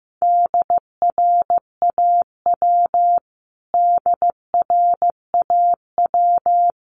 Morse code, heard as a keyed tone, about 15 wpm, 700 Hz. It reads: DRAW DRAW
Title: Top 500 Words Repeated 15wpm